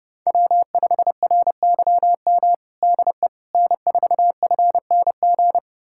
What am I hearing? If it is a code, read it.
Morse code, heard as a keyed tone, about 30 wpm, 700 Hz. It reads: W5RYM DE N4FNG
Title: USA Callsign DE Callsign - 5 Characters - 30wpm